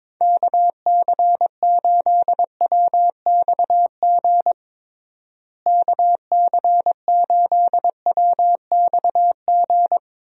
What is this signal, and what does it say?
Morse code, heard as a keyed tone, about 21 wpm, 700 Hz. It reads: KC8WXG KC8WXG
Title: USA Call Signs - 6 Characters - Repeated 2x - 22wpm